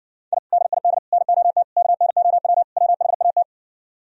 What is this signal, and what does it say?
Morse code, heard as a keyed tone, about 68 wpm, 700 Hz. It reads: I DID NOT KNOW WHAT